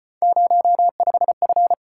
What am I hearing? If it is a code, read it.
Morse code, heard as a keyed tone, about 35 wpm, 700 Hz. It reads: 05F